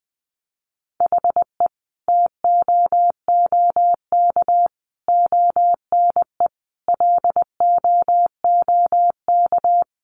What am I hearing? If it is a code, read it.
Morse code, heard as a keyed tone, about 20 wpm, 700 Hz. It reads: HE TOOK ONE LOOK
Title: Sentences from Top 600 Words 20wpm